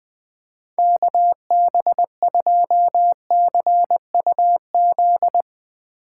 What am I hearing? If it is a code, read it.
Morse code, heard as a keyed tone, about 20 wpm, 700 Hz. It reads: KB2CUZ
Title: Speed Racing - USA 2x3 Call Signs 20wpm